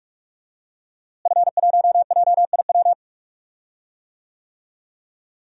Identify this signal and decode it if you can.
Morse code, heard as a keyed tone, about 46 wpm, 700 Hz. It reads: F1JIW
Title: DXCC-100 Call Signs - 5 Characters - Repeated 3x - 45wpm